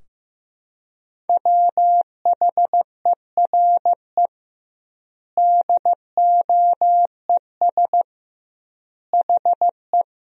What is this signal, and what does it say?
Morse code, heard as a keyed tone, about 15 wpm, 700 Hz. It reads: WHERE DOES HE
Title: 2x Word-Spacing - Sentences from Top 200 Words - 15wpm